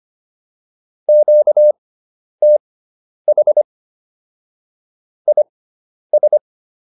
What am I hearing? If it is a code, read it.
Morse code, heard as a keyed tone, about 25 wpm, spaced as for 10 wpm, 600 Hz. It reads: QTH IS
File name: QSO310 10